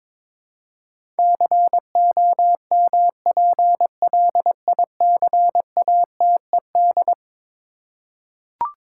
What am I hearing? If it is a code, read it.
Morse code, heard as a keyed tone, about 22 wpm, 700 Hz. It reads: COMPLICATED
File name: practice / 22wpm / Eleven Letter Words - 22wpm